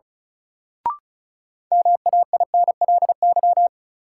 Morse code, heard as a keyed tone, about 36 wpm, 700 Hz. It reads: MAINLY
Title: Sentences from Top 1400 Words - New Word Drill Alt - 35wpm